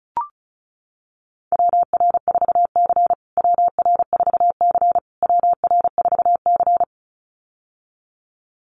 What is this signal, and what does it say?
Morse code, heard as a keyed tone, about 34 wpm, 700 Hz. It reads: WR4C WR4C WR4C
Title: USA Call Signs Repeated 3x 35wpm